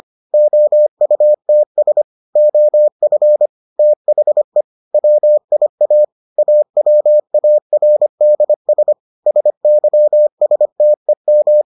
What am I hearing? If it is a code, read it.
Morse code, heard as a keyed tone, about 25 wpm, 600 Hz. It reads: OUTS OF THE WIA AWARDS SYSTEM